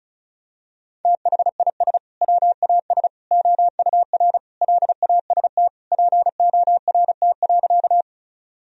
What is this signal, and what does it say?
Morse code, heard as a keyed tone, about 36 wpm, 700 Hz. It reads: THIS WAS OUR LAST PORT.
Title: Sentences from Top 1000 Words - Review Part 1 - 35wpm